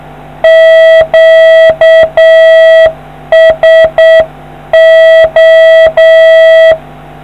Morse code, heard as a keyed tone, about 7 wpm, 650 Hz. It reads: QSO